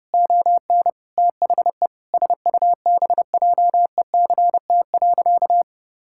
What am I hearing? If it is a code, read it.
Morse code, heard as a keyed tone, about 30 wpm, 700 Hz. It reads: ON THE SUBJECT.
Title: Sentences from Top 1000 Words - Review Part 6 - 30wpm